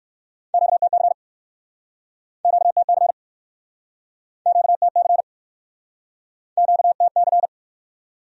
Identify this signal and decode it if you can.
Morse code, heard as a keyed tone, about 61 wpm, 700 Hz. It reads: QTC QTC QTC QTC